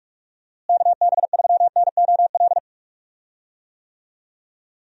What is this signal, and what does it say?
Morse code, heard as a keyed tone, about 41 wpm, 700 Hz. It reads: KB3NOL
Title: USA Call Signs - 6 Characters - Repeated 2x - 45wpm